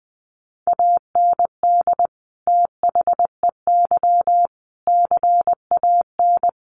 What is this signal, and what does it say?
Morse code, heard as a keyed tone, about 20 wpm, 700 Hz. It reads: AND THEY CAN
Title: Sentences from Top 1100 Words 20wpm